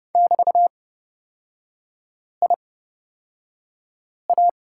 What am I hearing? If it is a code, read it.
Morse code, heard as a keyed tone, about 30 wpm, spaced as for 5 wpm, 700 Hz. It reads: <BT>IA
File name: Lesson 36e - BT - Three Characters 30wpm